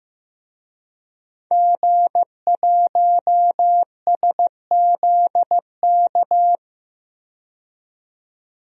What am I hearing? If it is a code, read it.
Morse code, heard as a keyed tone, about 15 wpm, 700 Hz. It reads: G1SZK
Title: DXCC-100 Call Signs - 5 Characters - 15wpm